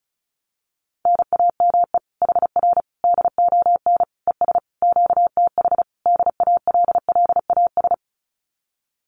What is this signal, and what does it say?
Morse code, heard as a keyed tone, about 36 wpm, 700 Hz. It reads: NAME HR DON ES QTH DALLAS
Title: Common QSO Phrases Encore 35wpm